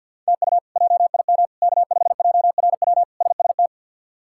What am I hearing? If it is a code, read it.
Morse code, heard as a keyed tone, about 45 wpm, 700 Hz. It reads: TU JIM K5JRW SST
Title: K1USN SST QSO Practice - 50wpm